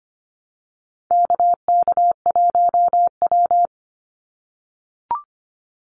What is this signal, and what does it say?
Morse code, heard as a keyed tone, about 25 wpm, 690 Hz. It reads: KK1W
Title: NA SOTA Activator Callsigns 25wpm